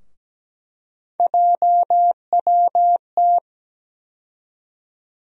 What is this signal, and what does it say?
Morse code, heard as a keyed tone, about 17 wpm, 700 Hz. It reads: JWT
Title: USA Call Sign Postfixes - 3 Letters 17wpm